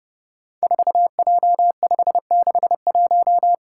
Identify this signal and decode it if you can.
Morse code, heard as a keyed tone, about 30 wpm, 700 Hz. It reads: 4J561
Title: Five Challenging Letter-Number Combinations 30wpm